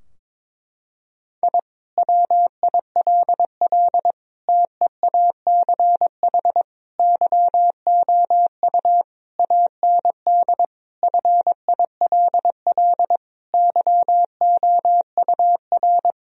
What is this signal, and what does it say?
Morse code, heard as a keyed tone, about 22 wpm, 700 Hz. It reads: I WILL TEACH YOU AND FILL YOUR
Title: Sentences from Top 1000 Words - Review Part 3 - 22wpm